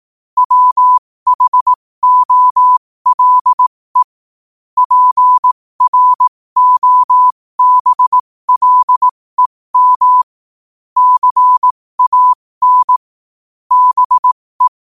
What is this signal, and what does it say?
Morse code, heard as a keyed tone, about 18 wpm, spaced as for 15 wpm, 1000 Hz. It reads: WHOLE PROBLEM CAN BE
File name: qotdfiles / 15wpm